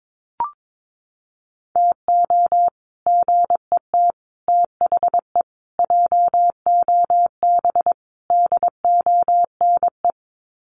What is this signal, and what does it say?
Morse code, heard as a keyed tone, about 22 wpm, 700 Hz. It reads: TO GET THE JOB DONE
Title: Sets of 5 Words Encore2 22wpm